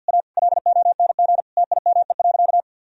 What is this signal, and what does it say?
Morse code, heard as a keyed tone, about 50 wpm, 700 Hz. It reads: A LONG TIME.
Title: Sentences from Top 1000 Words - Review Part 4 - 50wpm